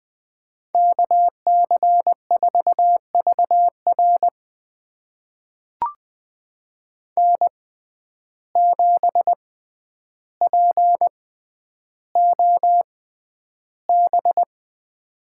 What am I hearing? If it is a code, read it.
Morse code, heard as a keyed tone, about 20 wpm, spaced as for 10 wpm, 700 Hz. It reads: KC4VR N7POB
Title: USA Call Signs - 6x Character Spacing - 20wpm